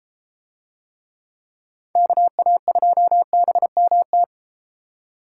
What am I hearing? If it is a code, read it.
Morse code, heard as a keyed tone, about 32 wpm, 700 Hz. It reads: KA2BMT